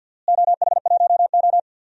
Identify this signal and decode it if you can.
Morse code, heard as a keyed tone, about 53 wpm, 700 Hz. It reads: OH1O
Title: DXCC-100 Call Signs - 4 Characters - Repeated 3x - 50wpm